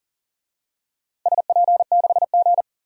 Morse code, heard as a keyed tone, about 37 wpm, 700 Hz. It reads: SP6G
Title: DXCC-100 Call Signs - 4 Characters - Repeated 3x - 40wpm